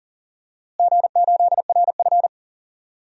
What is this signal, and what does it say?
Morse code, heard as a keyed tone, about 40 wpm, 700 Hz. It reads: G8RF
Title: DXCC-100 Call Signs - 4 Characters - Repeated 3x - 40wpm